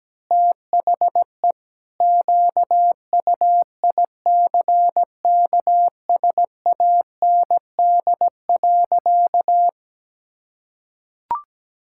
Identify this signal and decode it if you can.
Morse code, heard as a keyed tone, about 17 wpm, 700 Hz. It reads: THE QUICKSAND.
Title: Pangram Challenge v2 - 17wpm